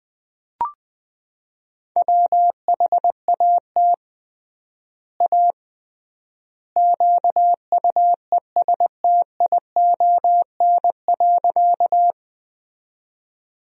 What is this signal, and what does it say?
Morse code, heard as a keyed tone, about 20 wpm, 700 Hz. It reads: WHAT A QUESTION.
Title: 3x Word-Spacing - Sentences from Top 400 Words - 20wpm